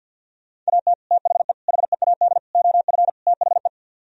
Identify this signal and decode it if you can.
Morse code, heard as a keyed tone, about 62 wpm, 700 Hz. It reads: AT THE HEAD OF THE
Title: Speed-Racing - Sets of 5 Words 50wpm